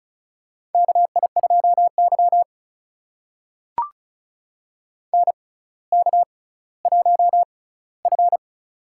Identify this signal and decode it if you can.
Morse code, heard as a keyed tone, about 36 wpm, spaced as for 18 wpm, 700 Hz. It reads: KI2Y NK1F
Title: USA Call Signs - 4 Characters - 6x Character Spacing - 35wpm